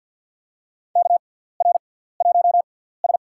Morse code, heard as a keyed tone, about 47 wpm, spaced as for 17 wpm, 700 Hz. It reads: KR1S
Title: USA Call Signs - 4 Characters - 6x Character Spacing - 50wpm